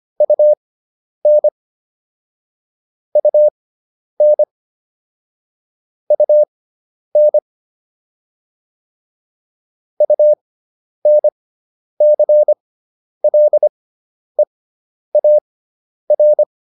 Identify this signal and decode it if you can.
Morse code, heard as a keyed tone, about 25 wpm, spaced as for 10 wpm, 600 Hz. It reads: UN UN UN UNCLEAR